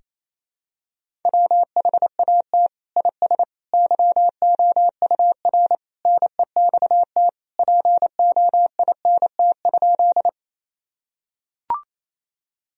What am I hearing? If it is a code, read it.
Morse code, heard as a keyed tone, about 28 wpm, 700 Hz. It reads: WHAT IS YOUR NEXT POINT?
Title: Sentences from Top 1000 Words - Review Part 1 - 28wpm